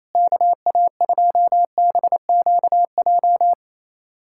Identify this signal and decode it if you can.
Morse code, heard as a keyed tone, about 28 wpm, 700 Hz. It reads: KA2BQJ
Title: USA Callsign DE Callsign - 6 Characters - 28wpm